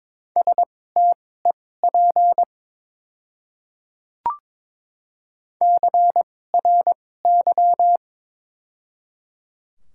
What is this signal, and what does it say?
Morse code, heard as a keyed tone, about 22 wpm, spaced as for 16 wpm, 700 Hz. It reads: STEP CRY
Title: 2x Character Spacing - Top 500 Words - 22wpm